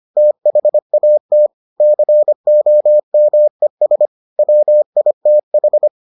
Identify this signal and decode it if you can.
Morse code, heard as a keyed tone, about 25 wpm, 600 Hz. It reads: THAT COMES WITH